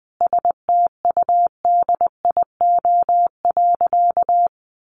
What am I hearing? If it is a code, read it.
Morse code, heard as a keyed tone, about 20 wpm, 700 Hz. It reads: STUDIO.
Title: Sentences from Top 1200 Words 20wpm